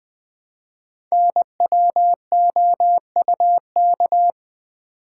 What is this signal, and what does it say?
Morse code, heard as a keyed tone, about 20 wpm, 700 Hz. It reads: NWOUK
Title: Code Groups - 5 Letters - 20wpm